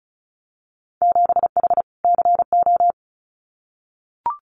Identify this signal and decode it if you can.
Morse code, heard as a keyed tone, about 36 wpm, 700 Hz. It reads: 7H CO
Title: Winter Field Day Exchanges Encore 35wpm